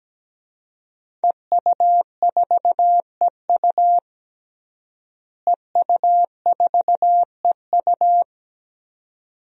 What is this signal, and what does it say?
Morse code, heard as a keyed tone, about 17 wpm, 700 Hz. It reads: EU4EU EU4EU
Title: DXCC-100 Call Signs - 5 Characters - Repeated 2x - 17wpm